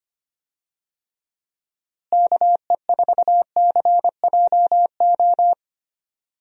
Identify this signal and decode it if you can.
Morse code, heard as a keyed tone, about 25 wpm, 700 Hz. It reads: KE4CJO
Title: USA Call Signs - 6 Characters - Repeated 2x - 25wpm